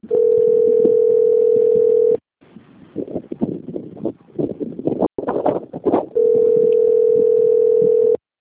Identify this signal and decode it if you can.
Ringback tone (North America), 440 + 480 Hz.
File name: rg-1600-unknown- - - .26978